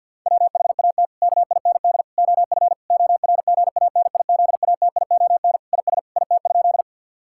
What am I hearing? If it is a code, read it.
Morse code, heard as a keyed tone, about 50 wpm, 700 Hz. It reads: WHAT KIND OF ORGANIZATION IS IT?
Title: Sentences from Top 1000 Words - Review Part 5 - 50wpm